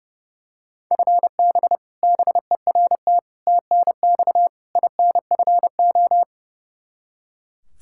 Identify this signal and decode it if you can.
Morse code, heard as a keyed tone, about 29 wpm, 700 Hz. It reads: FB BERT TNX INFO
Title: Practice QSO - Exchange 3 - More Info - 30wpm